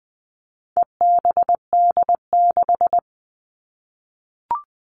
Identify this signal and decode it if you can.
Morse code, heard as a keyed tone, about 20 wpm, 700 Hz. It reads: EBD6
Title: All Cut Numbers - 4-Digit 20wpm